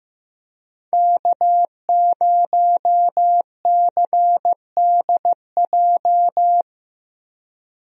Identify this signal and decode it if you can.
Morse code, heard as a keyed tone, about 15 wpm, 700 Hz. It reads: K0CDJ
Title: USA Call Signs - 5 Characters - Repeated 2x - 15wpm